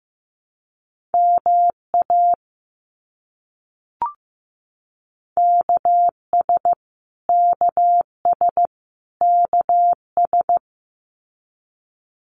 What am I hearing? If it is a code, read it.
Morse code, heard as a keyed tone, about 15 wpm, 700 Hz. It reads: MA KS KS KS
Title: US States Repeated 3x 15wpm